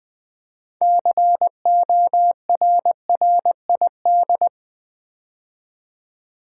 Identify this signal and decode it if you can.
Morse code, heard as a keyed tone, about 20 wpm, 700 Hz. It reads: CORRID
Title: Build A Word - 3 Letters At A Time - 8-Letter Words - 20wpm